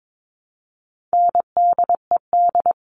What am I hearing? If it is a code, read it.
Morse code, heard as a keyed tone, about 22 wpm, 700 Hz. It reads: NDED
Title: All Cut Numbers - 4-Digit 22wpm